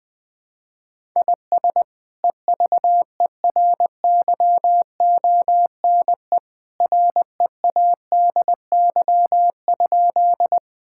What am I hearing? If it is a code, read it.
Morse code, heard as a keyed tone, about 21 wpm, 700 Hz. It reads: IS EVERYONE READY?
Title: Sentences from Top 1000 Words - Review Part 5 - 20wpm